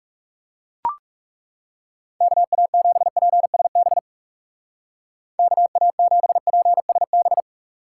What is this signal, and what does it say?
Morse code, heard as a keyed tone, about 36 wpm, 700 Hz. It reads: KA7PSB KA7PSB
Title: Speed Racing - USA Call Signs 30wpm